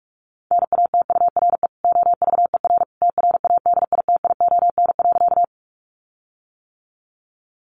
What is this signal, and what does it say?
Morse code, heard as a keyed tone, about 44 wpm, 700 Hz. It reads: NATURE OVER TRADITION.
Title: Sentences from Top 1500 Words 45wpm